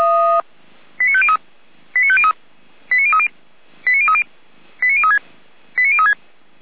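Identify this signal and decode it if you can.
Five-tone selective call (ZVEI), heard as quick runs of five tones, tones of about 70 ms. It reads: 89503 89503 89030 89030 89036 89036
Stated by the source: ZVEI five-tone selective call received on an SDR by feuerwehr-orlamuende fire brigade.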